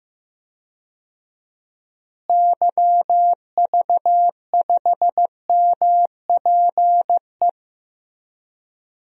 Morse code, heard as a keyed tone, about 15 wpm, 700 Hz. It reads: YV5MPE
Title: DXCC-100 Call Signs - 6 Characters - Repeated 3x - 15wpm